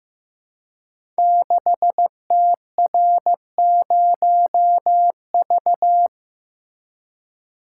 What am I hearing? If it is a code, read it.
Morse code, heard as a keyed tone, about 15 wpm, 700 Hz. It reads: 6TR0V
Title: Code Groups - 5 Letter:Numbers - 15wpm